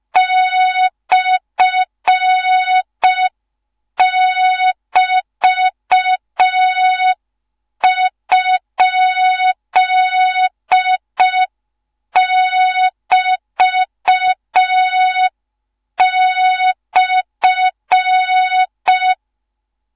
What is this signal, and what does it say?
Morse code, heard as a keyed tone, about 5 wpm, 3700 Hz. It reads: /<BT>?<BT>/
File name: Morze-HA2MN / hang / 13IRASJELEK1 HA2MN